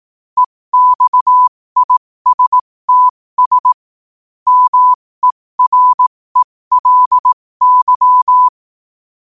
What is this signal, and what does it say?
Morse code, heard as a keyed tone, about 18 wpm, spaced as for 15 wpm, 1000 Hz. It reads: EXISTS MERELY
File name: qotdfiles / 15wpm